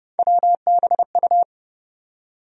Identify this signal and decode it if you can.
Morse code, heard as a keyed tone, about 32 wpm, 700 Hz. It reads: WBU
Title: USA Call Sign Postfixes - 3 Letters 30wpm